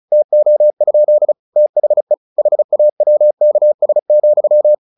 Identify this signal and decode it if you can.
Morse code, heard as a keyed tone, about 34 wpm, 600 Hz. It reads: TO? THE HAWKS,
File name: the old man and the sea / the old man and the sea 1 35wpm